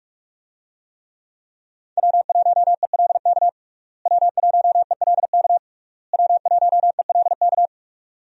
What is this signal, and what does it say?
Morse code, heard as a keyed tone, about 41 wpm, 700 Hz. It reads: W1ELK W1ELK W1ELK
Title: USA Call Signs - 5 Characters - Repeated 3x - 45wpm